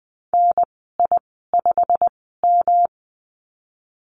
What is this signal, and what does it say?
Morse code, heard as a keyed tone, about 20 wpm, spaced as for 15 wpm, 700 Hz. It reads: NI5M
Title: 128 Lesson 21g - Number 6 - Call Signs 20wpm